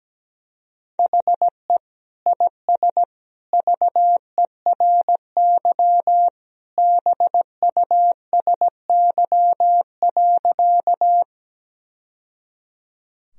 Morse code, heard as a keyed tone, about 17 wpm, 700 Hz. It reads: HE IS VERY BUSY.
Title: Sentences from Top 1000 Words - Review Part 3 - 17wpm